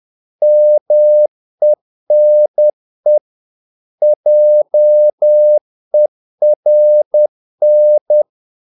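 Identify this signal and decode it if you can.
Morse code, heard as a keyed tone, about 10 wpm, 600 Hz. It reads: MENE JERN
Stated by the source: Morse practice of short Danish words at 10 wpm, from CW med Gnister.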